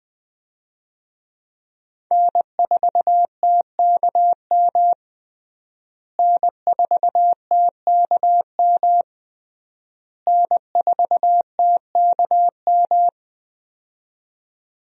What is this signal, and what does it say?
Morse code, heard as a keyed tone, about 20 wpm, 700 Hz. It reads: N4TKM N4TKM N4TKM
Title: USA Call Signs - 5 Characters - Repeated 3x - 20wpm